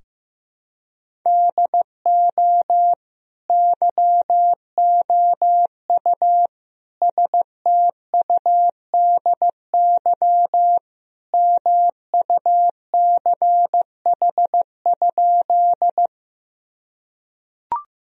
Morse code, heard as a keyed tone, about 15 wpm, 700 Hz. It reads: DO YOU STUDY MUCH?
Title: Sentences from Top 1000 Words - Review Part 1 - 15wpm